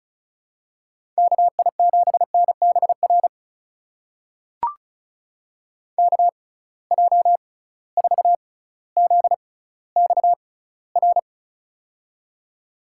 Morse code, heard as a keyed tone, about 36 wpm, spaced as for 18 wpm, 700 Hz. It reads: KI7NBR KJ4ZXR
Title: USA Call Signs - 6x Character Spacing - 35wpm